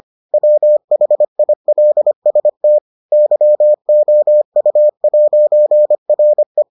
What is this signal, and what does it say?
Morse code, heard as a keyed tone, about 25 wpm, 600 Hz. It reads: WHILST YOU'RE